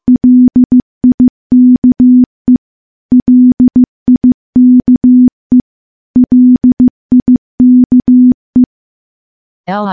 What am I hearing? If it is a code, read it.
Morse code, heard as a keyed tone, about 15 wpm, 250 Hz. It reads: LIKE LIKE LIKE
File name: morse / morse-training-15WPM